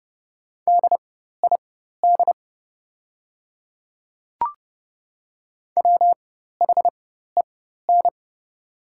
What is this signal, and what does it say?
Morse code, heard as a keyed tone, about 31 wpm, spaced as for 14 wpm, 690 Hz. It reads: DID WHEN
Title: 4x Character Spacing - Top 500 Words - 30wpm